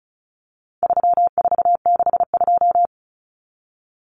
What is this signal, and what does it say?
Morse code, heard as a keyed tone, about 34 wpm, 700 Hz. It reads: 3462
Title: Common Cut Numbers - 4-Digit 35wpm